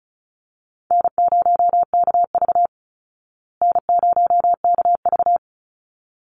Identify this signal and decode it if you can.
Morse code, heard as a keyed tone, about 34 wpm, 700 Hz. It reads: N0KV N0KV
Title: Call Signs Repeated 3x - 35wpm